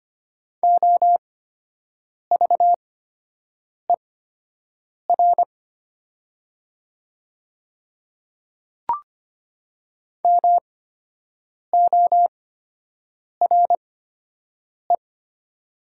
Morse code, heard as a keyed tone, about 25 wpm, spaced as for 7 wpm, 700 Hz. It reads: OVER MORE
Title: 8x Character Spacing - Top 100 Words - 25wpm